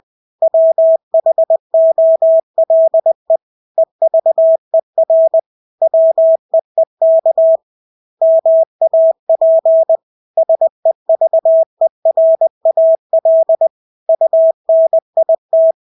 Morse code, heard as a keyed tone, about 20 wpm, 650 Hz. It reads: WHOLE EVER WEEK MAP SEVERAL UNIT
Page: Engelske ord 20 wpm